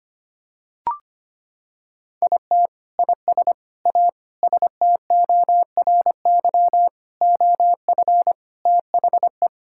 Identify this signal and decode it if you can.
Morse code, heard as a keyed tone, about 25 wpm, 700 Hz. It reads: IT IS A STORY OF THE
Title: Sentences from Top 1600 Words 25wpm